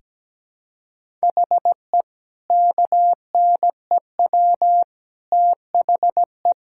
Morse code, heard as a keyed tone, about 17 wpm, 700 Hz. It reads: HE KNEW THE
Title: Sentences from Top 1000 Words - Review Part 5 - 17wpm